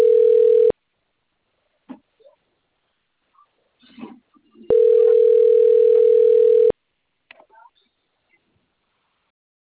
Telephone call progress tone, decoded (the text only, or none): ringback tone